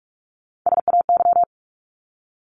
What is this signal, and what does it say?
Morse code, heard as a keyed tone, about 44 wpm, 700 Hz. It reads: SAY
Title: ING Suffix - Top 100 Words with Base Words 45wpm